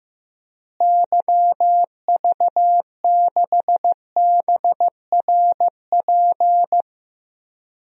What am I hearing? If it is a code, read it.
Morse code, heard as a keyed tone, about 15 wpm, 700 Hz. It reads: YV6BRP
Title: DXCC-100 Call Signs - 6 Characters - Repeated 2x - 15wpm